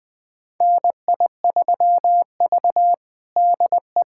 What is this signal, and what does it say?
Morse code, heard as a keyed tone, about 20 wpm, 700 Hz. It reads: NI3V DE
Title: USA Callsign DE Callsign - 4 Characters - 20wpm